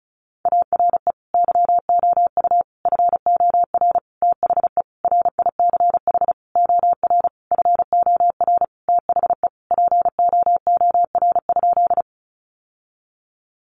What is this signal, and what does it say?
Morse code, heard as a keyed tone, about 34 wpm, 700 Hz. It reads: ARE YOU FOR THE RICH OR FOR THE POOR?
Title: Sentences from Top 1100 Words 35wpm